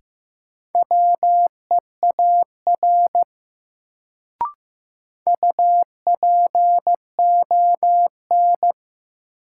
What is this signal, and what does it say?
Morse code, heard as a keyed tone, about 15 wpm, 700 Hz. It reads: WEAR UPON
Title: Sentences from Top 1000 Words - Review Part 6 - New Word Drill - 15wpm